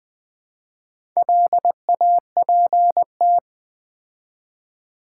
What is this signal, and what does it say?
Morse code, heard as a keyed tone, about 20 wpm, 700 Hz. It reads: LAPT
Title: Build A Word - 2 Letters At A Time - 6-Letter Words - 20wpm